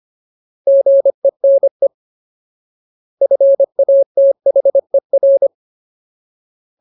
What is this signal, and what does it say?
Morse code, heard as a keyed tone, about 25 wpm, 550 Hz. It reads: GENE FATHER
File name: Ultimate Word List / 25WPM Ultimate